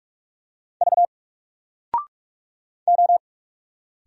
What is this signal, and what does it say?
Morse code, heard as a keyed tone, about 42 wpm, 700 Hz. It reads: V O
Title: Single Letter-Number 45wpm